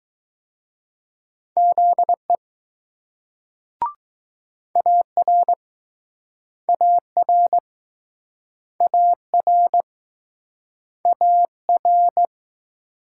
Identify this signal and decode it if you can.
Morse code, heard as a keyed tone, about 21 wpm, 700 Hz. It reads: ZE AR AR AR AR